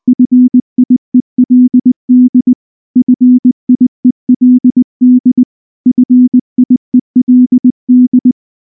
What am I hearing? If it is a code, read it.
Morse code, heard as a keyed tone, about 19 wpm, 250 Hz. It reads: FIELD FIELD FIELD
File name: morse / morse-training-20WPM-3